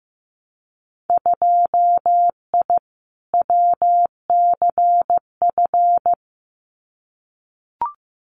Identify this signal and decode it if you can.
Morse code, heard as a keyed tone, about 15 wpm, 700 Hz. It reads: 2I WCF
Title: Winter Field Day Exchanges 15wpm